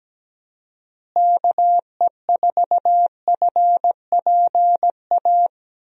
Morse code, heard as a keyed tone, about 17 wpm, 700 Hz. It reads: KE4FPA
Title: USA Call Signs - 6 Characters - Repeated 3x - 17wpm